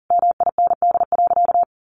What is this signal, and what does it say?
Morse code, heard as a keyed tone, about 40 wpm, 700 Hz. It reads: MIND.
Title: Sentences from Top 1100 Words 40wpm